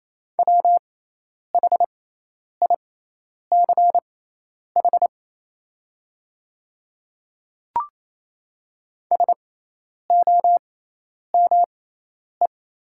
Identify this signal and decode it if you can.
Morse code, heard as a keyed tone, about 27 wpm, spaced as for 10 wpm, 690 Hz. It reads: WHICH SOME
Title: 6x Character Spacing - Top 100 Words - 28wpm